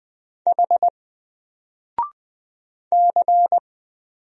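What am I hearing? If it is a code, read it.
Morse code, heard as a keyed tone, about 20 wpm, 700 Hz. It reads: H C
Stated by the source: Single letter and number Morse drill at 20 wpm, from N4LCD.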